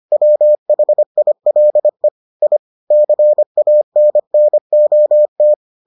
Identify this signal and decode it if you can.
Morse code, heard as a keyed tone, about 25 wpm, 600 Hz. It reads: WHILE I CANNOT